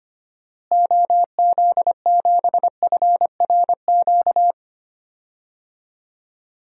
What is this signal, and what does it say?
Morse code, heard as a keyed tone, about 25 wpm, 700 Hz. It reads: OZ7FRQ
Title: DXCC-100 Call Signs - 6 Characters - Repeated 3x - 25wpm